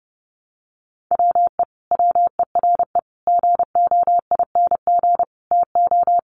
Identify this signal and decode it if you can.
Morse code, heard as a keyed tone, about 30 wpm, 700 Hz. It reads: WE WERE GOING TO
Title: Sets of 4 Words 30wpm